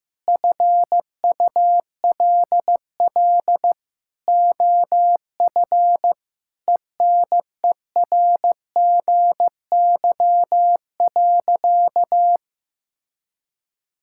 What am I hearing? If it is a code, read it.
Morse code, heard as a keyed tone, about 15 wpm, 700 Hz. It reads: FULL OF ENERGY.
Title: Sentences from Top 1000 Words - Review Part 6 - 15wpm